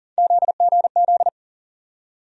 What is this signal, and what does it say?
Morse code, heard as a keyed tone, about 40 wpm, 700 Hz. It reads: ZGZ
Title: USA Call Sign Postfixes - 3 Letters 40wpm